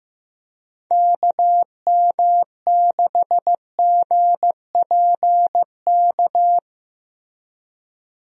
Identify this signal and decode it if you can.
Morse code, heard as a keyed tone, about 15 wpm, 700 Hz. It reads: KM6GPK
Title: USA Call Signs - 6 Characters - Repeated 2x - 15wpm